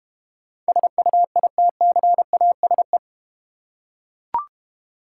Morse code, heard as a keyed tone, about 32 wpm, 700 Hz. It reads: SUITCASE